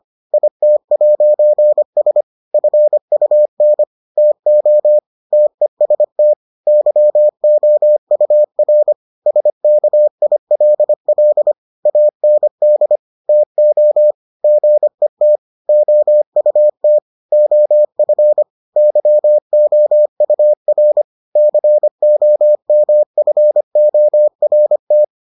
Morse code, heard as a keyed tone, about 25 wpm, 600 Hz. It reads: IT'S FUN TO TEST YOUR SKILL AND TO GET OUT OF YOUR COMFORT